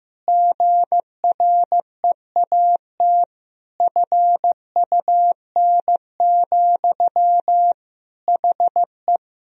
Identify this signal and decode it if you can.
Morse code, heard as a keyed tone, about 15 wpm, 700 Hz. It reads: GREAT FUN, HE
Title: Sentences from Top 1000 Words - Review Part 4 - 15wpm